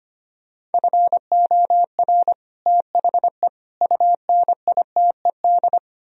Morse code, heard as a keyed tone, about 32 wpm, 690 Hz. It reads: FOR THE UNITED